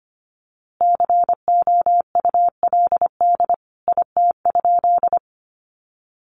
Morse code, heard as a keyed tone, about 25 wpm, 700 Hz. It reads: COULD IT?
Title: A Sound Of Thunder 25wpm